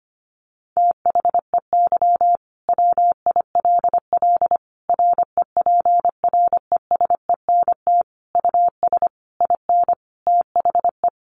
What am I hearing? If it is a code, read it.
Morse code, heard as a keyed tone, about 25 wpm, 700 Hz. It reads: THEY WILL REPRESENT US IN THE
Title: Sentences from Top 1200 Words 25wpm